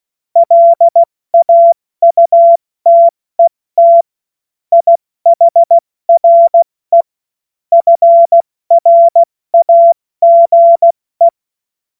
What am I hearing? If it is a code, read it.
Morse code, heard as a keyed tone, about 16 wpm, 680 Hz. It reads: LAUTET IHRE FRAGE
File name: Morsekurs / Witze / Witz 23. Mai 2023 70 F80